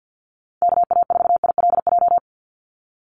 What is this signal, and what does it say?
Morse code, heard as a keyed tone, about 49 wpm, 700 Hz. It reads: KA4ILJ
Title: USA Call Signs 50wpm